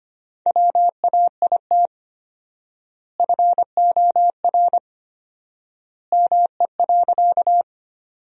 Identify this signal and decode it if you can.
Morse code, heard as a keyed tone, about 25 wpm, 700 Hz. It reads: WAIT FOR ME.
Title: 4x Word-Spacing - Sentences from Top 500 Words - 25wpm